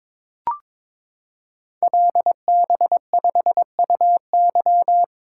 Morse code, heard as a keyed tone, about 22 wpm, 700 Hz. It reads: LB5UY
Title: Code Groups - 5 Letter:Numbers - 22wpm